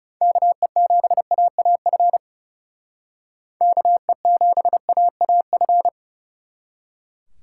Morse code, heard as a keyed tone, about 37 wpm, 700 Hz. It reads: KE7AAF KE7AAF
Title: Speed Racing - USA 2x3 Call Signs 30wpm